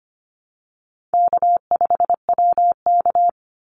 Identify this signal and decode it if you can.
Morse code, heard as a keyed tone, about 25 wpm, 700 Hz. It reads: K5WK
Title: Easy Call Signs Repeated 2x - 25wpm